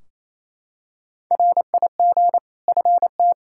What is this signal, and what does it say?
Morse code, heard as a keyed tone, about 28 wpm, 700 Hz. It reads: RIG FT
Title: Practice QSO - Exchange 2 - Info - 28wpm